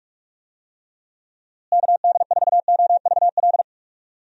Morse code, heard as a keyed tone, about 46 wpm, 700 Hz. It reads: KD4OVL
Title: USA Call Signs - 6 Characters - Repeated 3x - 45wpm